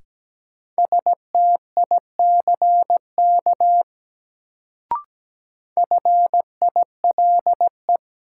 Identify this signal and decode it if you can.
Morse code, heard as a keyed tone, about 17 wpm, 700 Hz. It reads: STICK FILE
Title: Sentences from Top 1200 Words - New Word Drill Alt - 17wpm